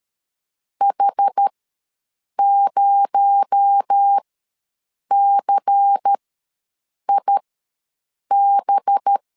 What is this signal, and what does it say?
Morse code, heard as a keyed tone, about 13 wpm, spaced as for 7 wpm, 800 Hz. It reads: H0CIB
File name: LEZIONI / 6 PKX0 / 6 PKX0 ws 12wpm 3448